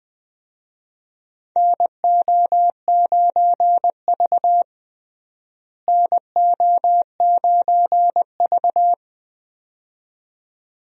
Morse code, heard as a keyed tone, about 20 wpm, 700 Hz. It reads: NO9V NO9V
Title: USA Call Signs - 4 Characters - Repeated 2x - 20wpm